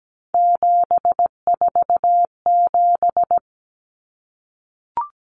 Morse code, heard as a keyed tone, about 17 wpm, 700 Hz. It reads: 747
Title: Numbers - 3-Digit 17wpm